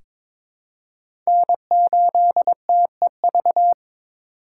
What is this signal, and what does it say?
Morse code, heard as a keyed tone, about 22 wpm, 700 Hz. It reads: N8TEV
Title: USA Call Signs - 5 Characters - 3x Character Spacing - 22wpm